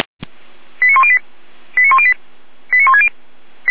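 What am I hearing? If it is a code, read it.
Five-tone selective call (ZVEI), heard as quick runs of five tones, tones of about 70 ms. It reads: 89108 89108 89160
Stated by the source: ZVEI five-tone selective call received via SDR.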